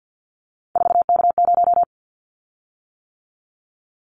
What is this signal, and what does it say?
Morse code, heard as a keyed tone, about 50 wpm, 700 Hz. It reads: 4K0
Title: Callsign Prefixes for Top 100 DXCC - 50wpm